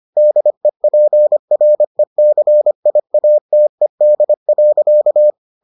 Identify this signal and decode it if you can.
Morse code, heard as a keyed tone, about 25 wpm, 600 Hz. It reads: DEPRECIATED.